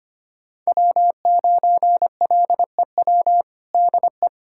Morse code, heard as a keyed tone, about 25 wpm, 700 Hz. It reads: W9LEW DE
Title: USA Callsign DE Callsign - 5 Characters - 25wpm